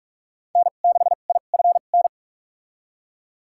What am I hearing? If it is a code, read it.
Morse code, heard as a keyed tone, about 44 wpm, spaced as for 32 wpm, 700 Hz. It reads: N6IFN
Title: USA Call Signs - 5 Characters - 2x Character Spacing - 45wpm